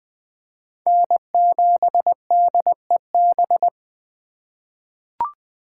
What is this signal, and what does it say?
Morse code, heard as a keyed tone, about 20 wpm, 700 Hz. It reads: N7DEB
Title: USA Call Signs - 5 Characters - 2x Character Spacing - 20wpm